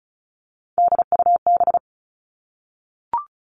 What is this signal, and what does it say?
Morse code, heard as a keyed tone, about 36 wpm, 700 Hz. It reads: DUB
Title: Three Challenging Letter-Number Combinations 35wpm